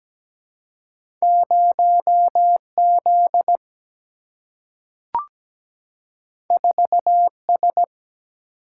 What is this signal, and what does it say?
Morse code, heard as a keyed tone, about 17 wpm, 700 Hz. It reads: 0 Z 4 S
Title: Single-Letter Single-Number 17wpm